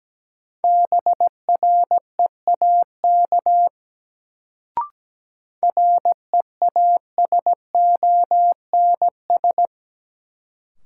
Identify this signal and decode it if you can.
Morse code, heard as a keyed tone, about 17 wpm, 700 Hz. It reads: BREAK REASONS
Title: Sets of 4 Words - New Word Drill - 17wpm